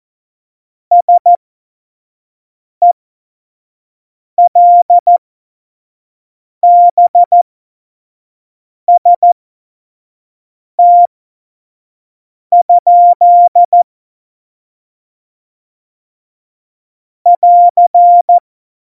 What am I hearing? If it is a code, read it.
Morse code, heard as a keyed tone, about 14 wpm, spaced as for 5 wpm, 700 Hz. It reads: SELBST? <AR>
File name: Morsekurs / Witze / Witz 08. September 2020, 25 F70